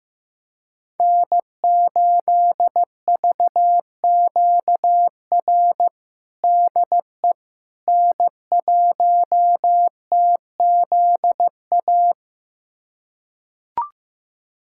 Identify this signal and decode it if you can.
Morse code, heard as a keyed tone, about 16 wpm, 700 Hz. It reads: N8VQR DE N1TZA
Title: USA Callsign DE Callsign - 5 Characters - 15wpm